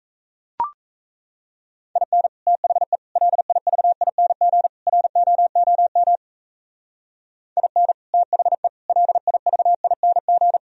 Morse code, heard as a keyed tone, about 34 wpm, 700 Hz. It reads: IN THE LIVING ROOM IN THE LIVING
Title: Speed-Racing - Sets of 4 Words 28wpm